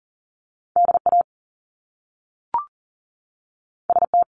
Morse code, heard as a keyed tone, about 39 wpm, 700 Hz. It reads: DA ST
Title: Top Two Letter Combinations in Words 40wpm